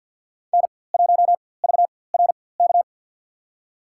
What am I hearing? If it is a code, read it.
Morse code, heard as a keyed tone, about 48 wpm, spaced as for 23 wpm, 700 Hz. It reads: N1VRK
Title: USA Call Signs - 4x Character Spacing - 50wpm